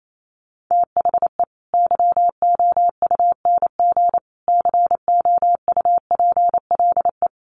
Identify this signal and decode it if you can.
Morse code, heard as a keyed tone, about 28 wpm, 700 Hz. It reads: THE YOUNG COUPLE
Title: Sentences from Top 1100 Words 28wpm